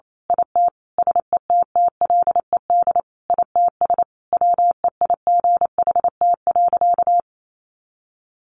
Morse code, heard as a keyed tone, about 28 wpm, 700 Hz. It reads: IT SETTLED ITS WEIGHT.
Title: A Sound Of Thunder 28wpm